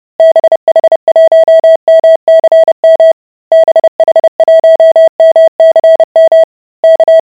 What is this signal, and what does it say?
Morse code, heard as a keyed tone, about 30 wpm, 650 Hz. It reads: BH1MCM BH1MCM K